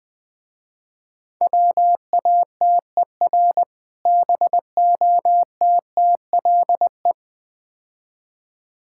Morse code, heard as a keyed tone, about 20 wpm, 700 Hz. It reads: WATER BOTTLE
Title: Difficult Words - Speed-Racing with Hints - 20wpm